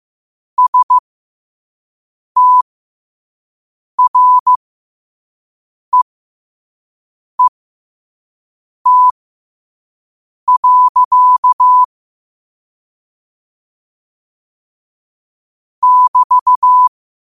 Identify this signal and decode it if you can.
Morse code, heard as a keyed tone, about 15 wpm, spaced as for 5 wpm, 1000 Hz. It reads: STREET. <BT>